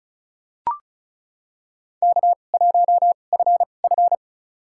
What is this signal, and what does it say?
Morse code, heard as a keyed tone, about 36 wpm, spaced as for 25 wpm, 700 Hz. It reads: K1FF
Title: USA Call Signs - 4 Characters - 2x Character Spacing - 35wpm